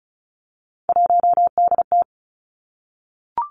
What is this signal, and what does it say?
Morse code, heard as a keyed tone, about 36 wpm, 700 Hz. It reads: 1DT
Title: Three Challenging Letter-Number Combinations 35wpm